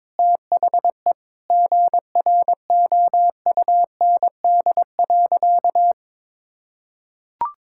Morse code, heard as a keyed tone, about 22 wpm, 700 Hz. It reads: THE GROUND.
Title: Sentences from Top 1600 Words 22wpm